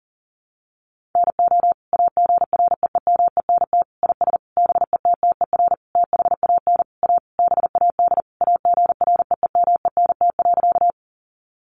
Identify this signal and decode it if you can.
Morse code, heard as a keyed tone, about 40 wpm, 700 Hz. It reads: NO AGREEMENT IS BETTER THAN A BAD AGREEMENT.
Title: Sentences from Top 1500 Words 40wpm